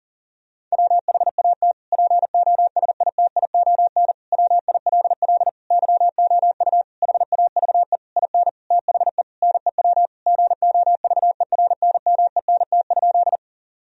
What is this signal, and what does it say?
Morse code, heard as a keyed tone, about 40 wpm, 700 Hz. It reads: WHAT POSITION WILL YOU HAVE IN THE NEW GOVERNMENT?
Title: Sentences from Top 1000 Words - Review Part 3 - 40wpm